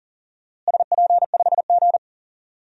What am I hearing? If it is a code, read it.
Morse code, heard as a keyed tone, about 40 wpm, 700 Hz. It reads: SP5G
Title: DXCC-100 Call Signs - 4 Characters - Repeated 3x - 40wpm